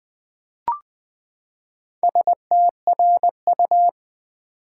Morse code, heard as a keyed tone, about 20 wpm, 700 Hz. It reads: STRU
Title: Build A Word - 4 Letters At A Time - 10-Letter Words - 20wpm